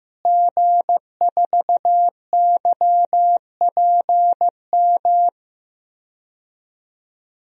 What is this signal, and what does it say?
Morse code, heard as a keyed tone, about 15 wpm, 700 Hz. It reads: G4YPM
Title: DXCC-100 Call Signs - 5 Characters - Repeated 2x - 15wpm